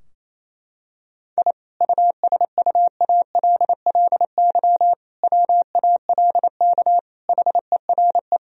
Morse code, heard as a keyed tone, about 28 wpm, 700 Hz. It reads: I USUALLY WALK HERE
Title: Sentences from Top 1000 Words - Review Part 3 - 28wpm